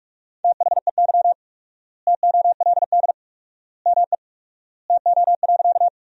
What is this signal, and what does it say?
Morse code, heard as a keyed tone, about 44 wpm, 700 Hz. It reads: THEY TOLD ME TO.
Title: 4x Word-Spacing - Sentences from Top 400 Words - 45wpm